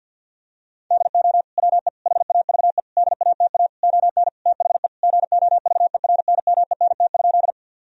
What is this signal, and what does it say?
Morse code, heard as a keyed tone, about 50 wpm, 700 Hz. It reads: DO WE HAVE DATA ON THE GOVERNMENT?
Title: Sentences from Top 1000 Words - Review Part 3 - 50wpm